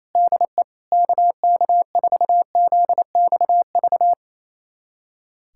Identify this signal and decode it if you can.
Morse code, heard as a keyed tone, about 28 wpm, 700 Hz. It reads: DE KK4ZXV
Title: USA Callsign DE Callsign 28wpm